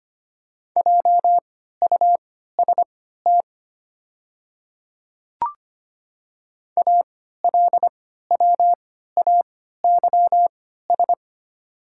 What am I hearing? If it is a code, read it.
Morse code, heard as a keyed tone, about 25 wpm, spaced as for 14 wpm, 700 Hz. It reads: JUST ALWAYS
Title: 3x Character Spacing - Top 300 Words - 25wpm